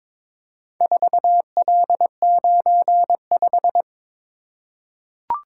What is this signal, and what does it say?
Morse code, heard as a keyed tone, about 22 wpm, 700 Hz. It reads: 4L95
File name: practice / 22wpm / Four Letter-Number Combinations - 22wpm